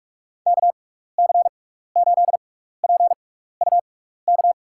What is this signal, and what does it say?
Morse code, heard as a keyed tone, about 41 wpm, spaced as for 15 wpm, 700 Hz. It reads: KC8PUK
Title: USA Call Signs - 6 Characters - 6x Character Spacing - 45wpm